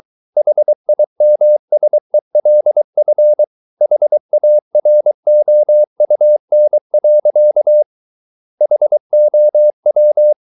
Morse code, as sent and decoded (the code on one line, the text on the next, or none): .... .. -- ... .-.. ..-. / .... .- .-. --- ..- -. .-.-.- / .... --- .--
HIMSELF HAROUN. HOW